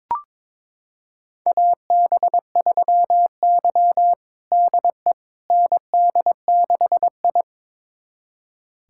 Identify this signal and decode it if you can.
Morse code, heard as a keyed tone, about 22 wpm, 700 Hz. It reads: AB3Y DE ND6I
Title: USA Callsign DE Callsign - 4 Characters - 22wpm